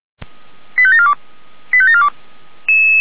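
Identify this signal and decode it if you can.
Five-tone selective call (ZVEI), heard as quick runs of five tones, tones of about 70 ms. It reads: 85742 85742
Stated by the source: ZVEI five-tone call received by SDR.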